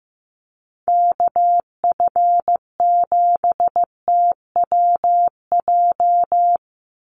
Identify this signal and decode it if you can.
Morse code, heard as a keyed tone, about 15 wpm, 700 Hz. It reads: KF7TWJ
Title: USA Call Signs 15wpm